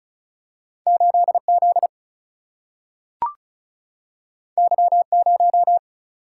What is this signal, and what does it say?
Morse code, heard as a keyed tone, about 36 wpm, 700 Hz. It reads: 8 Z Y 0
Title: Single-Letter Single-Number 35wpm